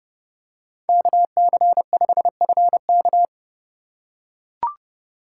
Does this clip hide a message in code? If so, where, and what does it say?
Yes: Morse code from 0.9 to 3.3 s, heard as a keyed tone, about 29 wpm, 700 Hz. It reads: KC5FK